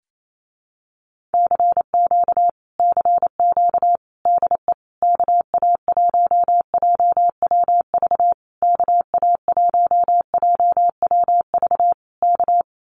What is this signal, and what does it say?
Morse code, heard as a keyed tone, about 28 wpm, 700 Hz. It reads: CQ CQ DE KA1JWV KA1JWV K
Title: Common QSO Phrases Encore 28wpm